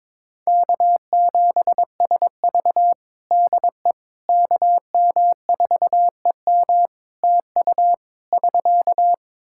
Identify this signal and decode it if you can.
Morse code, heard as a keyed tone, about 22 wpm, 700 Hz. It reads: K7SV DE KM4EM TU <SK>
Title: Practice QSO - Exchange 4 - Ending - 22wpm